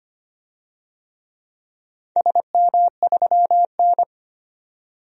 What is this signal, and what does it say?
Morse code, heard as a keyed tone, about 25 wpm, 700 Hz. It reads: SM3N
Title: DXCC-100 Call Signs - 4 Characters - 25wpm